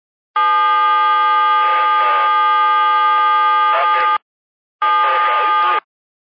DTMF, heard as telephone touch tones, tones of about 590 ms. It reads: 777
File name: qsorec SimplexLogic